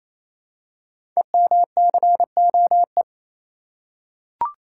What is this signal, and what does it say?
Morse code, heard as a keyed tone, about 28 wpm, 700 Hz. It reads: EMCOE
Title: Code Groups - 5 Letter:Numbers - 28wpm